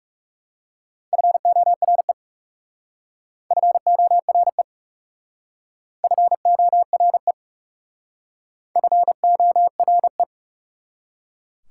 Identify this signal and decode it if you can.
Morse code, heard as a keyed tone, about 40 wpm, 690 Hz. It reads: FORE FORE FORE FORE